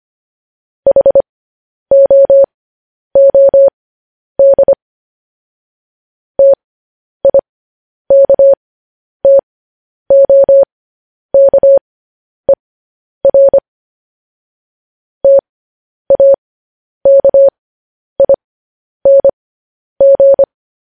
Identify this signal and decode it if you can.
Morse code, heard as a keyed tone, about 25 wpm, spaced as for 10 wpm, 550 Hz. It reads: HOOD TIKTOKER TAKING
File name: podcast / News headlines-1x-25x10